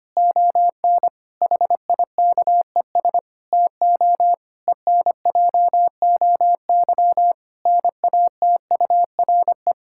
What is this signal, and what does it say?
Morse code, heard as a keyed tone, about 25 wpm, 700 Hz. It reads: ON HIKES TO ENJOY NATURE
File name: practice / 25wpm / Difficult Sentences with Hints - 25wpm